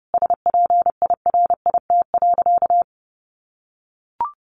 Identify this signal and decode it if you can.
Morse code, heard as a keyed tone, about 28 wpm, 700 Hz. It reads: SPIRIT.
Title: Sentences from Top 1400 Words 30wpm